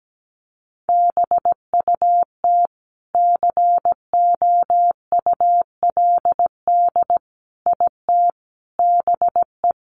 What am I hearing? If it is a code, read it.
Morse code, heard as a keyed tone, about 17 wpm, 700 Hz. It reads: BUT COULD IT BE
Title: Sentences from Top 1200 Words 17wpm